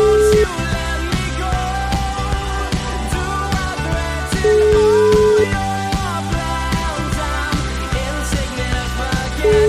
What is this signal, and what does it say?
Ringback tone (Europe, 425 Hz), with noise.